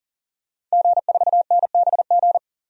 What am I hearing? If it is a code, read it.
Morse code, heard as a keyed tone, about 40 wpm, 700 Hz. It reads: G4NBG
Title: DXCC-100 Call Signs - 5 Characters - 40wpm